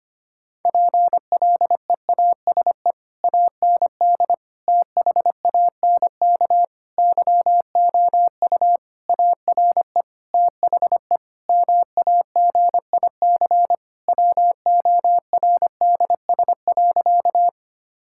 Morse code, heard as a keyed tone, about 25 wpm, 700 Hz. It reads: PLEASE AND THANK YOU ARE THE MAGIC WORDS.
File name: practice / 25wpm / Difficult Sentences with Hints - 25wpm